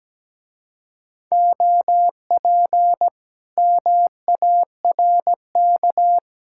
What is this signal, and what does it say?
Morse code, heard as a keyed tone, about 17 wpm, 700 Hz. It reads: OP MARK
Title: Speed-Racing Common QSO Phrases Encore - 17wpm